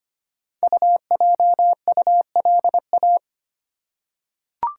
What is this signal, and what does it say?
Morse code, heard as a keyed tone, about 25 wpm, 700 Hz. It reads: UJULA
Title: Code Groups - 5 Letters - 25wpm